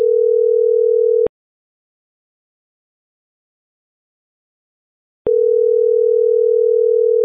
Ringback tone (North America), 440 + 480 Hz.